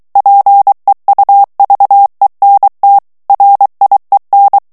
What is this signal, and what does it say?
Morse code, heard as a keyed tone, about 23 wpm, 800 Hz. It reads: PEUVENT RIEN
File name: morse mp3 / FRANCAIS TEXTE CLAIR 2 VITESSE 22 MOTS MINUTES